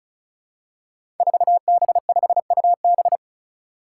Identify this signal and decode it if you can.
Morse code, heard as a keyed tone, about 34 wpm, 700 Hz. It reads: 4B5UB
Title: Five Challenging Letter-Number Combinations 35wpm